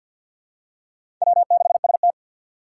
Morse code, heard as a keyed tone, about 52 wpm, 700 Hz. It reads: W6ST
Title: USA Call Signs - 4x Character Spacing - 50wpm